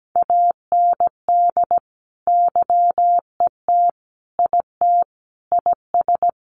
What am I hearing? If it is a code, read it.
Morse code, heard as a keyed tone, about 17 wpm, 700 Hz. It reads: AND YET IT IS
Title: Sentences from Top 1400 Words 17wpm